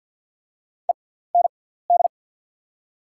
Morse code, heard as a keyed tone, about 51 wpm, spaced as for 17 wpm, 690 Hz. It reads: END
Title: 6x Character Spacing - Top 300 Words - 50wpm